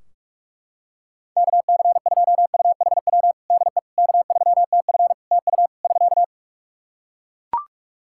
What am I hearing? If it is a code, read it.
Morse code, heard as a keyed tone, about 44 wpm, 700 Hz. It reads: KC2UHW DE K3TF TU <SK>
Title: Practice QSO - Exchange 4 - Ending - 45wpm